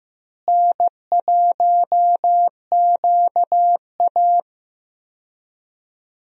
Morse code, heard as a keyed tone, about 15 wpm, 700 Hz. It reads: N1QA
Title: USA Call Signs - 4 Characters - Repeated 2x - 15wpm